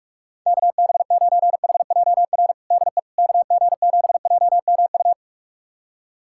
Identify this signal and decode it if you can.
Morse code, heard as a keyed tone, about 46 wpm, 700 Hz. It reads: KB9HJR DE KG7JMU
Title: USA Callsign DE Callsign - 6 Characters - 45wpm